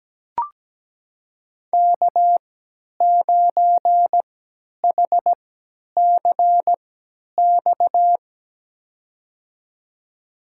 Morse code, heard as a keyed tone, about 17 wpm, spaced as for 10 wpm, 700 Hz. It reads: K9HCX
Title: USA Call Signs - 3x Character Spacing - 17wpm